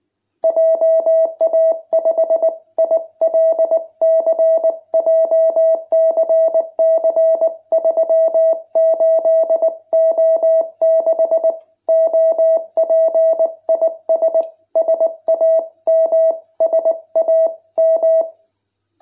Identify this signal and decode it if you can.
Morse code, heard as a keyed tone, about 19 wpm, 660 Hz. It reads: JA5ILC JCC38O6 OP IS SAM SAM